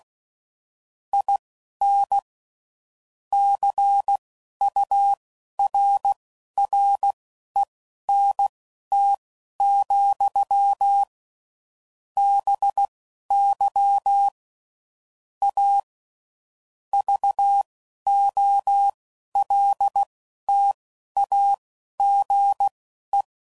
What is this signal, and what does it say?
Morse code, heard as a keyed tone, about 16 wpm, spaced as for 11 wpm, 800 Hz. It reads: IN CURRENT, BY A VOLTAGE
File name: CW Code Groups @10WPM / Technical Subjects / XL